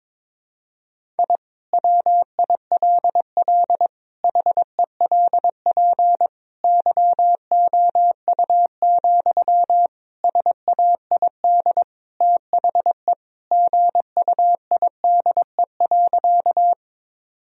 Morse code, heard as a keyed tone, about 21 wpm, 700 Hz. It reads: I WILL HELP YOU, SAID THE GUIDE.
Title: Sentences from Top 1000 Words - Review Part 3 - 22wpm